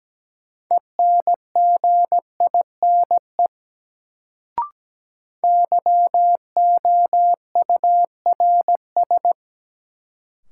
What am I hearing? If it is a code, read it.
Morse code, heard as a keyed tone, about 17 wpm, 700 Hz. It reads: ENGINE YOURS
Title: Sentences from Top 1500 Words - New Word Drill Alt - 17wpm